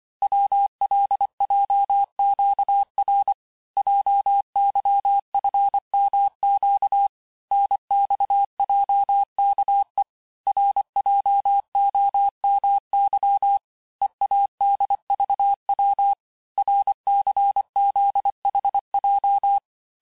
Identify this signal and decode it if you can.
Morse code, heard as a keyed tone, about 25 wpm, 800 Hz. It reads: WLJQR JYFMQ NXJKE RJOMY EADVW RCZHJ
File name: texte-chiffre-vitesse-23mpm